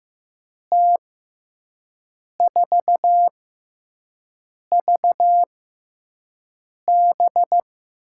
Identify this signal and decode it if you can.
Morse code, heard as a keyed tone, about 15 wpm, 700 Hz. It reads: T 4 V B
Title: USA Call Signs - 6x Character Spacing - 15wpm